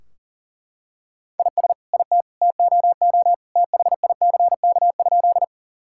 Morse code, heard as a keyed tone, about 40 wpm, 700 Hz. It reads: IS IT TOO THICK?
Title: Sentences from Top 1600 Words 40wpm